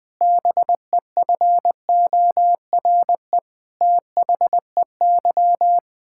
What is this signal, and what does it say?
Morse code, heard as a keyed tone, about 20 wpm, 700 Hz. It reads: BEFORE THEY
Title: Pangram Challenge v2 - 20wpm